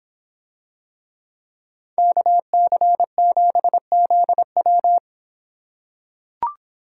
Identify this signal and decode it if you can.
Morse code, heard as a keyed tone, about 25 wpm, 700 Hz. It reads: KC7ZW